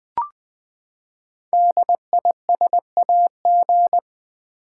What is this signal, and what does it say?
Morse code, heard as a keyed tone, about 20 wpm, 700 Hz. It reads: DISAG
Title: Build A Word - 5 Letters At A Time - 12-Letter Words - 20wpm